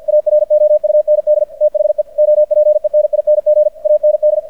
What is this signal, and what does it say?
Morse code, heard as a keyed tone, about 50 wpm, 600 Hz. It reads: AROUND THE OPENING AND
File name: archived / CRSnet